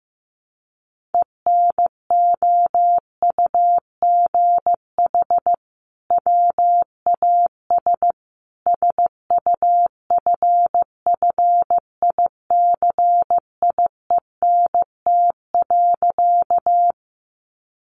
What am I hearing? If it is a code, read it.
Morse code, heard as a keyed tone, about 15 wpm, 700 Hz. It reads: ENOUGH WAS SUFFICIENT.
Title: Sentences from Top 1400 Words 15wpm